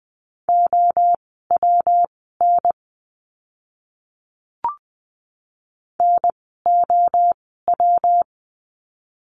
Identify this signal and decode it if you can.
Morse code, heard as a keyed tone, about 20 wpm, spaced as for 15 wpm, 700 Hz. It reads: OWN NOW
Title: 146 Lesson 24f - Letter y - Words 20wpm